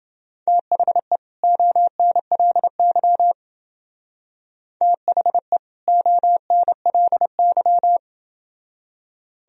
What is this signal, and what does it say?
Morse code, heard as a keyed tone, about 28 wpm, 700 Hz. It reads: THE ONLY THE ONLY